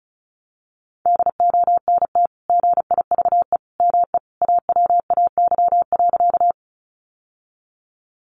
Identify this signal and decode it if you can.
Morse code, heard as a keyed tone, about 34 wpm, 700 Hz. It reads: DONT GIVE ME AWAY.
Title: Sentences from Top 600 Words 35wpm